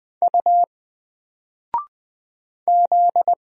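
Morse code, heard as a keyed tone, about 20 wpm, 700 Hz. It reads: U Z
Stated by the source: Single letter and number Morse drill at 20 wpm, from N4LCD.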